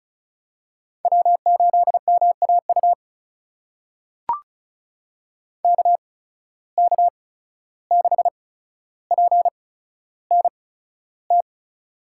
Morse code, heard as a keyed tone, about 34 wpm, spaced as for 14 wpm, 700 Hz. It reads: W8MAU KK6PNT
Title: USA Call Signs - 8x Character Spacing - 35wpm